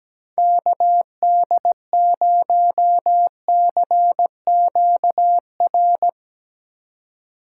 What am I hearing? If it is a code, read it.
Morse code, heard as a keyed tone, about 17 wpm, 700 Hz. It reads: KD0CQR
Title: USA Call Signs - 6 Characters - 4x Character Spacing - 17wpm